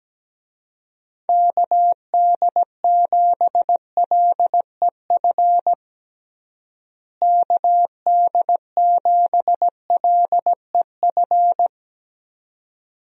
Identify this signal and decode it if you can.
Morse code, heard as a keyed tone, about 17 wpm, 700 Hz. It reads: KD7LEF KD7LEF
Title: USA Call Signs - 6 Characters - Repeated 2x - 17wpm